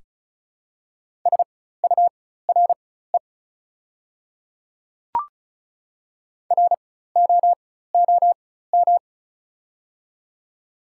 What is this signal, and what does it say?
Morse code, heard as a keyed tone, about 36 wpm, spaced as for 16 wpm, 700 Hz. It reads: SURE ROOM
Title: 4x Character Spacing - Top 300 Words - 35wpm